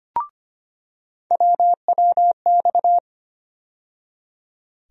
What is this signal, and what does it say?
Morse code, heard as a keyed tone, about 25 wpm, 700 Hz. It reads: WWX
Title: USA Call Sign Postfixes - 3 Letters 25wpm